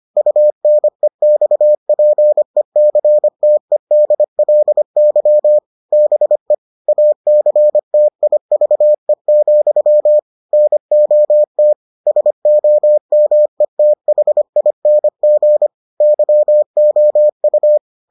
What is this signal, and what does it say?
Morse code, heard as a keyed tone, about 25 wpm, 600 Hz. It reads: UNEXPECTEDLY BE ACTIVE, NOT SOMETHING YOU